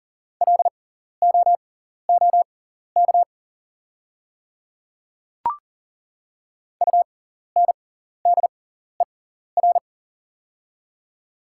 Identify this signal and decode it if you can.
Morse code, heard as a keyed tone, about 40 wpm, spaced as for 14 wpm, 700 Hz. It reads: LOOK UNDER
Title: 6x Character Spacing - Top 200 Words - 40wpm